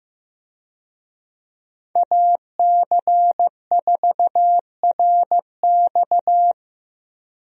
Morse code, heard as a keyed tone, about 15 wpm, 700 Hz. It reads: AC4RX
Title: USA Call Signs - 5 Characters - 15wpm